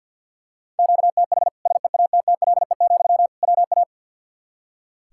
Morse code, heard as a keyed tone, about 46 wpm, 700 Hz. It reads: QTH SEATTLE, WA
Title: Practice QSO - Exchange 1 - Beginning - 50wpm